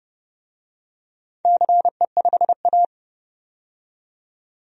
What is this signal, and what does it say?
Morse code, heard as a keyed tone, about 30 wpm, 700 Hz. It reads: CE5A
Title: DXCC-100 Call Signs - 4 Characters - 30wpm